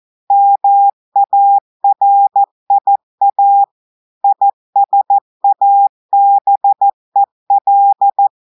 Morse code, heard as a keyed tone, about 14 wpm, 800 Hz. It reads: MARIA ISABEL
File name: audios / portugais / portugais texte en clair 1 vitesse 14 mots minutes